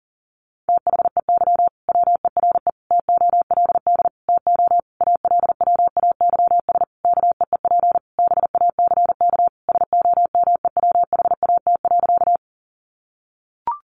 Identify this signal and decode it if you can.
Morse code, heard as a keyed tone, about 40 wpm, 700 Hz. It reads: THEY WERE TOLD TO ALWAYS KEEP BACK SOMEWHAT.
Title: Sentences from Top 1300 Words 40wpm